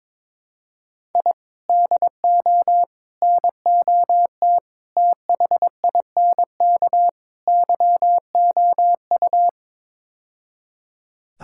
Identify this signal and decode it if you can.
Morse code, heard as a keyed tone, about 22 wpm, 700 Hz. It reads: I DO NOT THINK YOU
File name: practice / 22wpm / Build A Phrase - 5-Word Phrases - 22wpm